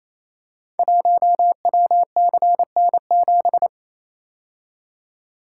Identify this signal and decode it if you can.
Morse code, heard as a keyed tone, about 27 wpm, 700 Hz. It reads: 1WCN7
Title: Code Groups - 5 Letter:Numbers - 28wpm